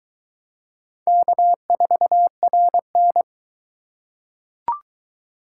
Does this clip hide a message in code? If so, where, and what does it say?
Yes: Morse code from 1.1 to 3.2 s, heard as a keyed tone, about 23 wpm, 700 Hz. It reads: K4RN